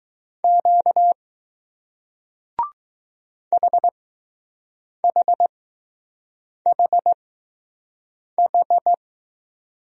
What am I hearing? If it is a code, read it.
Morse code, heard as a keyed tone, about 20 wpm, spaced as for 6 wpm, 700 Hz. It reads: Q HHHH